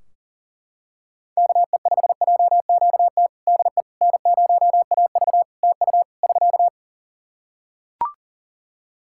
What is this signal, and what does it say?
Morse code, heard as a keyed tone, about 40 wpm, 700 Hz. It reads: KE5JQT DE N0AV TU <SK>
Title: Practice QSO - Exchange 4 - Ending - 40wpm